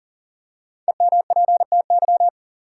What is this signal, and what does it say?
Morse code, heard as a keyed tone, about 40 wpm, 700 Hz. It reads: EMPTY
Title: Five Letter Words - 40wpm